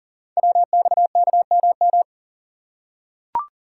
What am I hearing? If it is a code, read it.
Morse code, heard as a keyed tone, about 38 wpm, 690 Hz. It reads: WXKMM
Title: Code Groups - 5 Letters - 40wpm